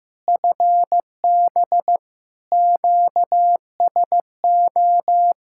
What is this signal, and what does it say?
Morse code, heard as a keyed tone, about 15 wpm, 700 Hz. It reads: FB QSO
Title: Practice QSO - Exchange 4 - Ending - 15wpm